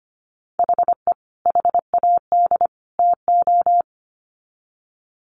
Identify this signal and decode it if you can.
Morse code, heard as a keyed tone, about 25 wpm, 700 Hz. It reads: HE HAD TO
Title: Sets of 3 Words 25wpm